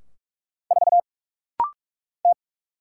Morse code, heard as a keyed tone, about 44 wpm, 700 Hz. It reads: 4 T
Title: Single Letter-Number - Rapid-Fire 45wpm